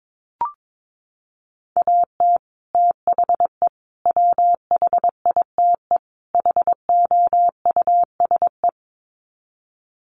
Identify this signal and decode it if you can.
Morse code, heard as a keyed tone, about 22 wpm, 700 Hz. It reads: AT THE WHITE HOUSE
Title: Sets of 4 Words 22wpm